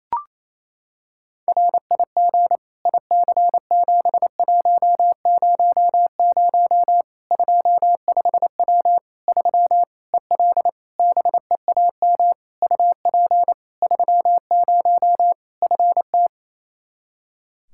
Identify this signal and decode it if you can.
Morse code, heard as a keyed tone, about 27 wpm, 700 Hz. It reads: RIG IC7100 25W 3 EL BEAM UP 30 FT
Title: Practice QSO - Exchange 2 - Info - 28wpm